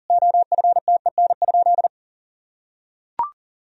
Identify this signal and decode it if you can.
Morse code, heard as a keyed tone, about 40 wpm, 700 Hz. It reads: OFTEN?
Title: Sentences from Top 1000 Words - Review Part 4 - 40wpm